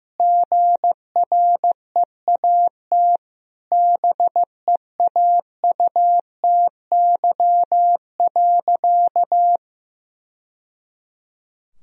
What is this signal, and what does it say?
Morse code, heard as a keyed tone, about 15 wpm, 700 Hz. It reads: GREAT BEAUTY.
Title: Sentences from Top 1000 Words - Review Part 2 - 15wpm